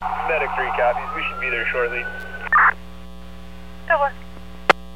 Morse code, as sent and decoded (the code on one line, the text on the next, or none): none